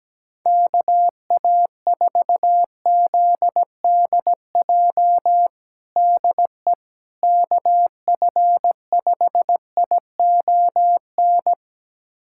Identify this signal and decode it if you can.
Morse code, heard as a keyed tone, about 17 wpm, 700 Hz. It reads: KA4ZDJ DE KF5ION
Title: USA Callsign DE Callsign - 6 Characters - 17wpm